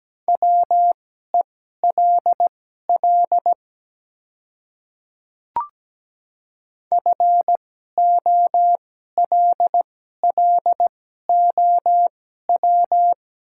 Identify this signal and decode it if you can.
Morse code, heard as a keyed tone, about 17 wpm, spaced as for 12 wpm, 700 Hz. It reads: WELL FOLLOW
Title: 2x Character Spacing - Top 300 Words - 17wpm